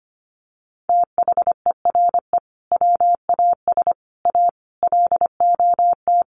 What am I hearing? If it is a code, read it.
Morse code, heard as a keyed tone, about 25 wpm, 700 Hz. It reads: THERE WAS A LOT
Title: Sets of 5 Words 25wpm